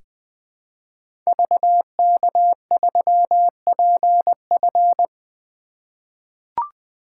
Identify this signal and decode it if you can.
Morse code, heard as a keyed tone, about 20 wpm, 700 Hz. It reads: VK3PF
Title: 2023 SOTA Activator Call Signs - 20wpm